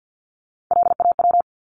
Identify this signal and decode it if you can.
Morse code, heard as a keyed tone, about 50 wpm, 700 Hz. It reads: LAW
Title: ES:S Suffix - Top 100 Words with Base Words 50wpm